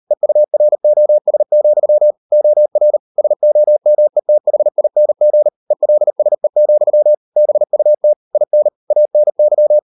Morse code, heard as a keyed tone, about 38 wpm, 600 Hz. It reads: EUROS, OR SOMETHING ELSE, BUT IN ANY